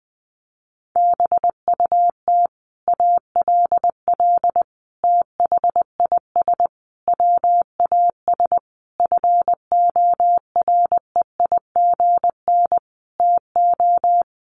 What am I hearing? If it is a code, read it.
Morse code, heard as a keyed tone, about 20 wpm, 700 Hz. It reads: BUT ALL THIS WAS FOREIGN TO
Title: Sentences from Top 1400 Words 20wpm